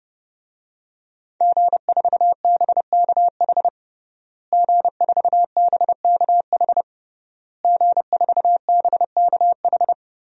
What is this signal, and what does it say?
Morse code, heard as a keyed tone, about 30 wpm, 700 Hz. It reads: G4BKH G4BKH G4BKH
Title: DXCC-100 Call Signs - 5 Characters - Repeated 3x - 30wpm